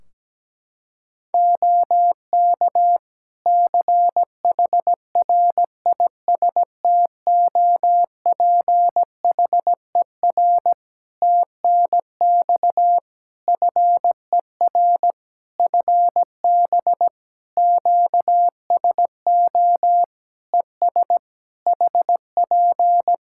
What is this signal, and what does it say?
Morse code, heard as a keyed tone, about 17 wpm, 700 Hz. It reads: OK CHRISTOPHER TNX FER FB QSO ES HP
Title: Practice QSO - Exchange 4 - Ending - 17wpm